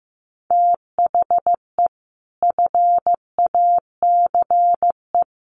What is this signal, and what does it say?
Morse code, heard as a keyed tone, about 15 wpm, 700 Hz. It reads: THE FACE
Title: Sets of 4 Words 15wpm